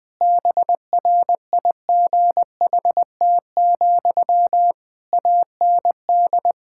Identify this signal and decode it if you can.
Morse code, heard as a keyed tone, about 21 wpm, 700 Hz. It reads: BRIGHT, AND
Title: Sentences from Top 1600 Words 20wpm